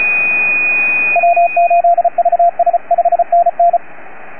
Morse code, heard as a keyed tone, about 34 wpm, 690 Hz, with radio noise. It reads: W8VS 5NN